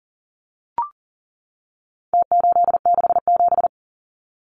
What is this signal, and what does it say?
Morse code, heard as a keyed tone, about 40 wpm, 700 Hz. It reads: T867
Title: Numbers - Contest Style - 4-Digit 40wpm